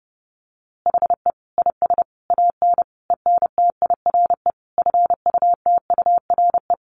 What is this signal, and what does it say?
Morse code, heard as a keyed tone, about 30 wpm, 700 Hz. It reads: HE IS AN ENTIRE FUTURE
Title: A Sound Of Thunder 30wpm